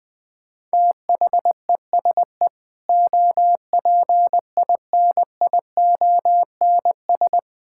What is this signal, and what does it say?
Morse code, heard as a keyed tone, about 21 wpm, 700 Hz. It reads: THESE OPINIONS
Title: Sentences with 8-Word Phrases - 20wpm